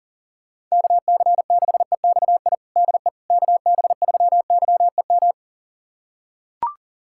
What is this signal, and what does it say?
Morse code, heard as a keyed tone, about 40 wpm, 700 Hz. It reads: KC6EXI DE KB3YEM
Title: USA Callsign DE Callsign 40wpm